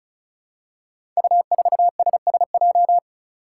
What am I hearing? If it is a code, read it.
Morse code, heard as a keyed tone, about 36 wpm, 700 Hz. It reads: U4SSJ
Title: Five Challenging Letter-Number Combinations 35wpm